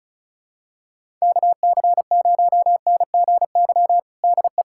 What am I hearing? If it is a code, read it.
Morse code, heard as a keyed tone, about 36 wpm, 700 Hz. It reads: KC0NGY DE
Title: USA Callsign DE Callsign - 6 Characters - 35wpm